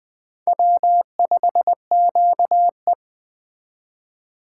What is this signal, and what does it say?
Morse code, heard as a keyed tone, about 21 wpm, 700 Hz. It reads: W5QE
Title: Speed Racing - USA 1x2 Call Signs 17wpm